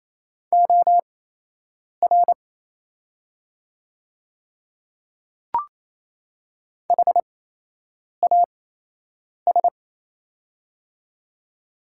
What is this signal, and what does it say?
Morse code, heard as a keyed tone, about 28 wpm, spaced as for 8 wpm, 690 Hz. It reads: OR HAS
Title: 8x Character Spacing - Top 300 Words - 28wpm